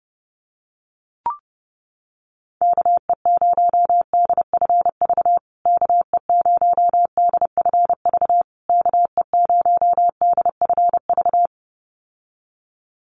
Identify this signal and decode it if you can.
Morse code, heard as a keyed tone, about 30 wpm, 700 Hz. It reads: KE0DFV KE0DFV KE0DFV
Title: USA Call Signs Repeated 3x 30wpm